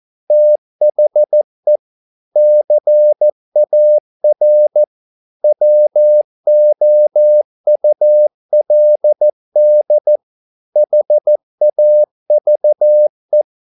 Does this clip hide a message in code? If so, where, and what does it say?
Yes: Morse code from 0.3 to 13.4 s, heard as a keyed tone, about 14 wpm, 600 Hz. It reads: THE CAR WOULD HAVE